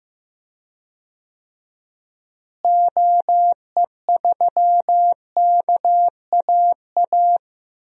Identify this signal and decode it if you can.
Morse code, heard as a keyed tone, about 15 wpm, 700 Hz. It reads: OE3KAA
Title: DXCC-100 Call Signs - 6 Characters - Repeated 2x - 15wpm